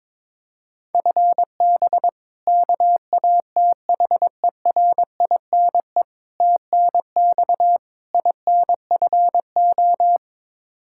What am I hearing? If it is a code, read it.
Morse code, heard as a keyed tone, about 22 wpm, 700 Hz. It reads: FB KATHERINE TNX INFO
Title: Practice QSO - Exchange 3 - More Info - 22wpm